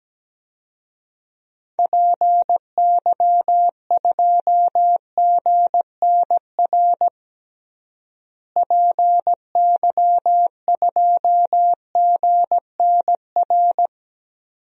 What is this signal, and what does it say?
Morse code, heard as a keyed tone, about 17 wpm, 700 Hz. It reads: PY2GNR PY2GNR
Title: DXCC-100 Call Signs - 6 Characters - Repeated 2x - 17wpm